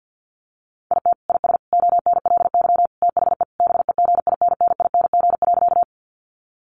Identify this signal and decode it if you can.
Morse code, heard as a keyed tone, about 50 wpm, 700 Hz. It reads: IT IS ONLY THE BEGINNING.
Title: Sentences from Top 800 Words 50wpm